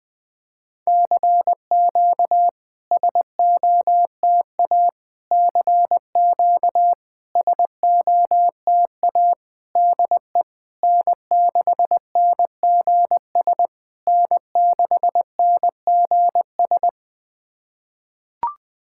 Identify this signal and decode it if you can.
Morse code, heard as a keyed tone, about 20 wpm, 700 Hz. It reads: CQ SOTA CQ SOTA DE N6NGS N6NGS
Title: SOTA QSO Practice 20wpm